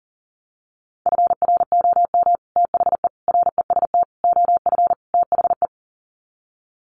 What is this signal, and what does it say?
Morse code, heard as a keyed tone, about 37 wpm, 700 Hz. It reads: FROM THE REST OF THE
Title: Sets of 5 Words 40wpm